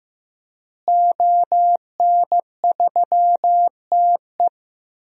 Morse code, heard as a keyed tone, about 15 wpm, 700 Hz. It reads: ON3TE
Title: DXCC-100 Call Signs - 5 Characters - Repeated 2x - 15wpm